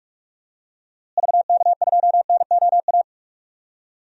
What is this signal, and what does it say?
Morse code, heard as a keyed tone, about 46 wpm, 700 Hz. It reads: VK2NOA
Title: DXCC-100 Call Signs - 6 Characters - Repeated 2x - 45wpm